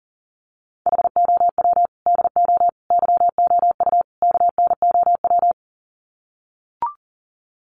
Morse code, heard as a keyed tone, about 40 wpm, 700 Hz. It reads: HOW DO YOU KNOW
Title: Sets of 4 Words 40wpm